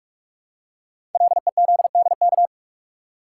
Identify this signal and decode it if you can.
Morse code, heard as a keyed tone, about 44 wpm, 690 Hz. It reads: LEZDK
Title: Code Groups - 5 Letters - 45wpm